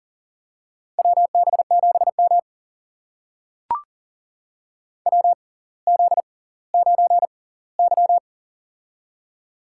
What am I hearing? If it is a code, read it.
Morse code, heard as a keyed tone, about 40 wpm, spaced as for 20 wpm, 700 Hz. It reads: WB7M WZ9Y
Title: USA Call Signs - 4 Characters - 6x Character Spacing - 40wpm